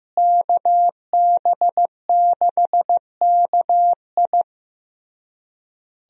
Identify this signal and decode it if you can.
Morse code, heard as a keyed tone, about 16 wpm, 700 Hz. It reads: KB6KI
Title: USA Call Signs - 5 Characters - Repeated 3x - 15wpm